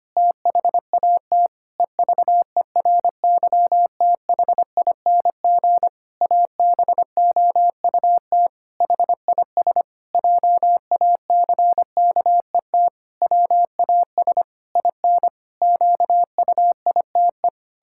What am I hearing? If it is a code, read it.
Morse code, heard as a keyed tone, about 25 wpm, 700 Hz. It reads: THAT EVERYTHING ABOUT HIS JACKET WAS IN QUITE
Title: Pangram Challenge v2 - 25wpm